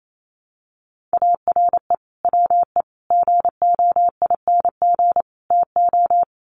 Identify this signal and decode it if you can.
Morse code, heard as a keyed tone, about 28 wpm, 700 Hz. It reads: ARE WE GOING TO
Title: Sets of 4 Words 28wpm